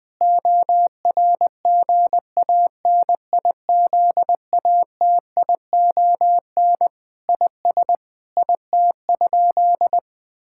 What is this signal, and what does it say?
Morse code, heard as a keyed tone, about 19 wpm, 700 Hz. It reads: ORGANIZATION IS IT?
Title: Sentences from Top 1000 Words - Review Part 5 - 20wpm